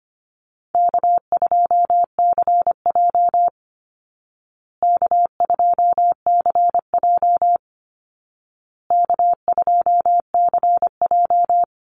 Morse code, heard as a keyed tone, about 25 wpm, 700 Hz. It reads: K2CJ K2CJ K2CJ
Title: Easy Call Signs Repeated 3x - 25wpm